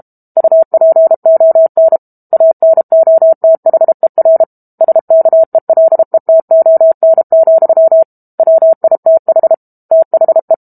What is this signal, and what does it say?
Morse code, heard as a keyed tone, about 32 wpm, 650 Hz. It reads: UPON ANOTHER SKELETON, WITH THE
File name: Book Two / 06-Chapter 6 - 32 WPM